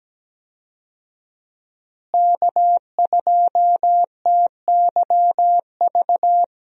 Morse code, heard as a keyed tone, about 17 wpm, 700 Hz. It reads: K2TYV
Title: USA Call Signs - 5 Characters - Repeated 2x - 17wpm